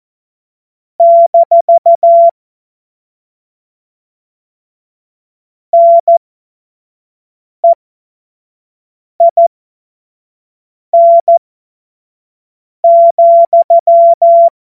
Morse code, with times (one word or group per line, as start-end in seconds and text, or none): 1.0-2.3 -
5.7-14.5 NEIN,